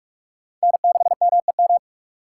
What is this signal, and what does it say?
Morse code, heard as a keyed tone, about 44 wpm, 700 Hz. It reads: N6MEM